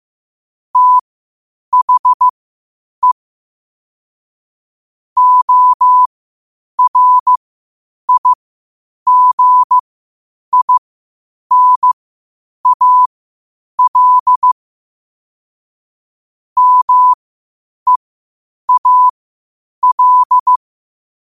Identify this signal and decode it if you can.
Morse code, heard as a keyed tone, about 15 wpm, spaced as for 8 wpm, 1000 Hz. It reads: THE ORIGINAL MEAL
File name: qotdfiles / 8wpm